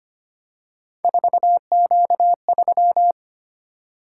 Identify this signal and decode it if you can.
Morse code, heard as a keyed tone, about 25 wpm, 700 Hz. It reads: 4Q3
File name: Three Letter-Number Combinations - 25wpm